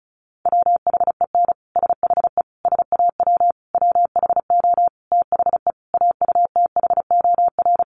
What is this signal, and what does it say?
Morse code, heard as a keyed tone, about 34 wpm, 700 Hz. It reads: WHEN SHE SAW WHO THE AUTHOR
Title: Sentences from Top 1200 Words 35wpm